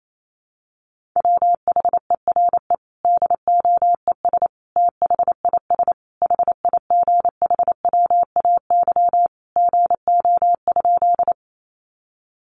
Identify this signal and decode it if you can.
Morse code, heard as a keyed tone, about 28 wpm, 700 Hz. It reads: WHERE DOES THIS HIGHWAY GO?
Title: Sentences from Top 1400 Words 28wpm